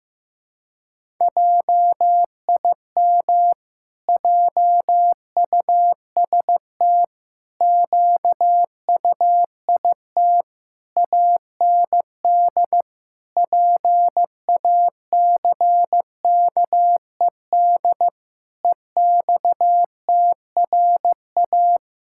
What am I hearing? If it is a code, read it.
Morse code, heard as a keyed tone, about 15 wpm, 700 Hz. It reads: JIM JUST QUIT AND PACKED EXTRA
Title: Pangram Challenge v2 - 15wpm